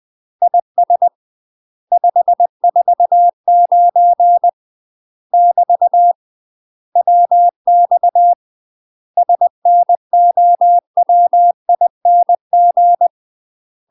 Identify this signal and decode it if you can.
Morse code, heard as a keyed tone, about 20 wpm, 700 Hz. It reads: IS 549 <BT> WX SNOWING
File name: UK MASTER QSOs TRAINING / 20WPM / UK MASTER QSOs 0015